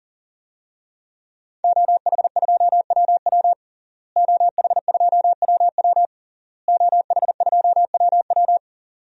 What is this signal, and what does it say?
Morse code, heard as a keyed tone, about 40 wpm, 700 Hz. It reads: OH2WW OH2WW OH2WW
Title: DXCC-100 Call Signs - 5 Characters - Repeated 3x - 40wpm